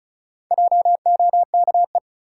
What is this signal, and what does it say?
Morse code, heard as a keyed tone, about 36 wpm, 700 Hz. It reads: JOKE
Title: Four Letter Words - 35wpm